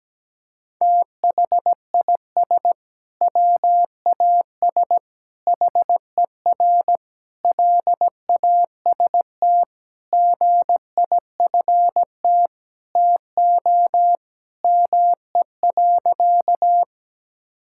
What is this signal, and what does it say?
Morse code, heard as a keyed tone, about 17 wpm, 700 Hz. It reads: THIS WAS HER LAST GIFT TO ME.
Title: Sentences from Top 1000 Words - Review Part 6 - 17wpm